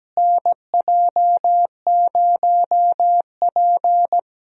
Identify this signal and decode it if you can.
Morse code, heard as a keyed tone, about 17 wpm, 700 Hz. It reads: NJ0P
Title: USA Call Signs - 4 Characters - Repeated 3x - 17wpm